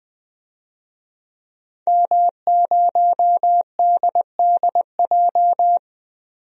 Morse code, heard as a keyed tone, about 20 wpm, 700 Hz. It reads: M0DDJ
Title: DXCC-100 Call Signs - 5 Characters - 20wpm